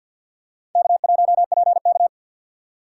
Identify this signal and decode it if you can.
Morse code, heard as a keyed tone, about 52 wpm, 700 Hz. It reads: K1PK
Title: USA Call Signs - 4 Characters - 2x Character Spacing - 50wpm